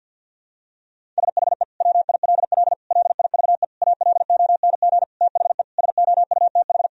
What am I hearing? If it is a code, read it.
Morse code, heard as a keyed tone, about 52 wpm, 700 Hz. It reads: SHE WILL LIVE ALONG THE SOUTH
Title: Sentences from Top 1000 Words - Review Part 4 - 50wpm